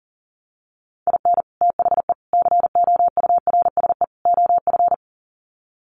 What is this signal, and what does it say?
Morse code, heard as a keyed tone, about 37 wpm, 700 Hz. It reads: IN THE COURSE OF
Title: Sets of 4 Words 40wpm